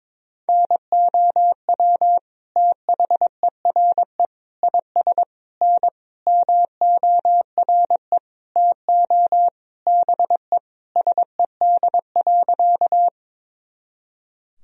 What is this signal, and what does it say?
Morse code, heard as a keyed tone, about 22 wpm, 700 Hz. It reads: NOW THERE IS N MORE TO BE SED.
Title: Sentences from Top 100 Words and Abbreviations 22wpm